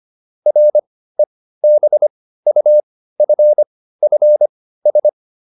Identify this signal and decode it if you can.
Morse code, heard as a keyed tone, about 25 wpm, spaced as for 15 wpm, 600 Hz. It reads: REBUFFS